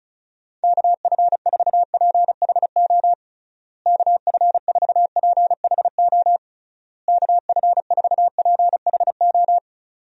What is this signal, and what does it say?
Morse code, heard as a keyed tone, about 36 wpm, 700 Hz. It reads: KF4PHO KF4PHO KF4PHO
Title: USA Call Signs - 6 Characters - Repeated 3x - 35wpm